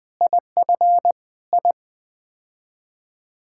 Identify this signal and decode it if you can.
Morse code, heard as a keyed tone, about 20 wpm, 700 Hz. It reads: IF I
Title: Build A Phrase - 5-Word Phrases - 20wpm